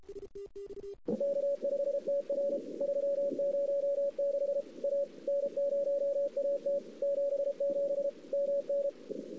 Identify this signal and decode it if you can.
Morse code, heard as a keyed tone, about 33 wpm, 560 Hz. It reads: K5TR 306 A N0AT 77 MN
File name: CW / 2008-ss-cw-k5tr-op-n5rz -02:00